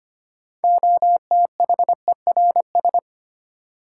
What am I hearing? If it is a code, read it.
Morse code, heard as a keyed tone, about 25 wpm, 700 Hz. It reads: OTHERS
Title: Sentences from Top 1000 Words - Review Part 2 - New Word Drill - 25wpm